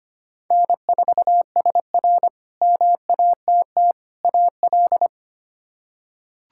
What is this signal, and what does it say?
Morse code, heard as a keyed tone, about 25 wpm, 700 Hz. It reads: N4SR MATT AL
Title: K1USN SST QSO Practice - 25wpm